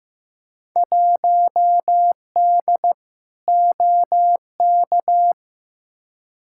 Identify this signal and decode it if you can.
Morse code, heard as a keyed tone, about 15 wpm, 700 Hz. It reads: 1D OK
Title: Field Day Exchanges - 15wpm